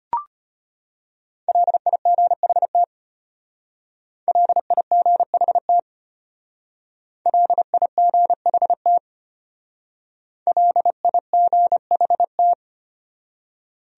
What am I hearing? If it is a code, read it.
Morse code, heard as a keyed tone, about 31 wpm, 700 Hz. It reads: LIGHT LIGHT LIGHT LIGHT
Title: Speed-Racing Top 500 Words 25wpm